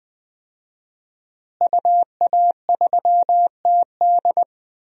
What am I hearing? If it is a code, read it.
Morse code, heard as a keyed tone, about 20 wpm, 700 Hz. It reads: UA3TD
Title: DXCC-100 Call Signs - 5 Characters - Repeated 3x - 20wpm